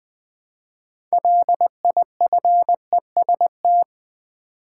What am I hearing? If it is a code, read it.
Morse code, heard as a keyed tone, about 20 wpm, 700 Hz. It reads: LIFEST
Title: Build A Word - 3 Letters At A Time - 9-Letter Words - 20wpm